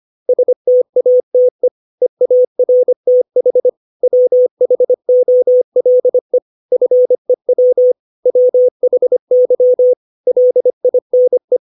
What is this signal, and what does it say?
Morse code, heard as a keyed tone, about 25 wpm, 500 Hz. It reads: STATE EARTH WHOLE FEW WHY LINE
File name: 632 words / 632 Words 25WPM